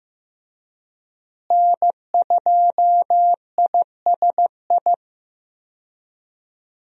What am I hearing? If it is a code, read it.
Morse code, heard as a keyed tone, about 15 wpm, 700 Hz. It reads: N2ISI
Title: USA Call Signs - 5 Characters - Repeated 3x - 15wpm